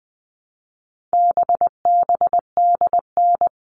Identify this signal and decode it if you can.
Morse code, heard as a keyed tone, about 20 wpm, 700 Hz. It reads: BBDN
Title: All Cut Numbers - 4-Digit 20wpm